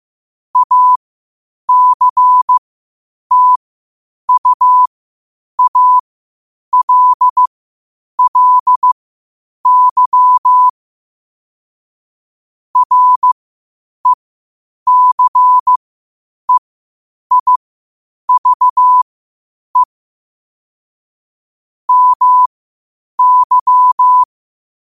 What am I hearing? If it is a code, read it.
Morse code, heard as a keyed tone, about 15 wpm, spaced as for 8 wpm, 1000 Hz. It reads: ACTUALLY RECEIVE MY
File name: qotdfiles / 8wpm